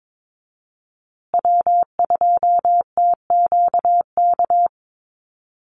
Morse code, heard as a keyed tone, about 22 wpm, 700 Hz. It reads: W2TQK
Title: USA Call Signs Repeated 3x 22wpm